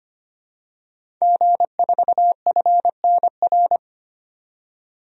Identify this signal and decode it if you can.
Morse code, heard as a keyed tone, about 25 wpm, 700 Hz. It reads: G4FNR
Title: DXCC-100 Call Signs - 5 Characters - 25wpm